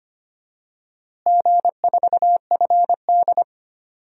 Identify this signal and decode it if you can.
Morse code, heard as a keyed tone, about 25 wpm, 700 Hz. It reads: G4FD
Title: DXCC-100 Call Signs - 4 Characters - Repeated 3x - 25wpm